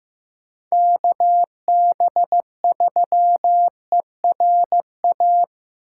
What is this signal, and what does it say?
Morse code, heard as a keyed tone, about 15 wpm, 700 Hz. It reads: KB3ERA
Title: USA Call Signs - 6 Characters - 8x Character Spacing - 15wpm